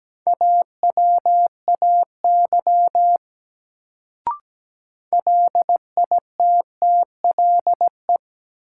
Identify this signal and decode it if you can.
Morse code, heard as a keyed tone, about 17 wpm, 700 Hz. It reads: AWAY LITTLE
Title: Sets of 2 Words - New Word Drill - 17wpm